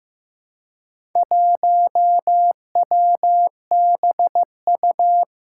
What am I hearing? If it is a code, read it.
Morse code, heard as a keyed tone, about 15 wpm, 700 Hz. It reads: 1WBU
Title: Four Challenging Letter-Number Combinations 15wpm